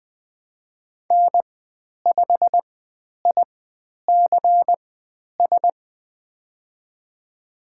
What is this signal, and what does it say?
Morse code, heard as a keyed tone, about 20 wpm, spaced as for 10 wpm, 700 Hz. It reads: N5ICS
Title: 102 Lesson 17g - Letter u - Call Signs 20wpm